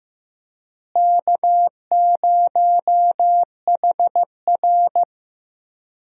Morse code, heard as a keyed tone, about 15 wpm, 700 Hz. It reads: K0HR
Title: USA Call Signs - 4 Characters - Repeated 3x - 15wpm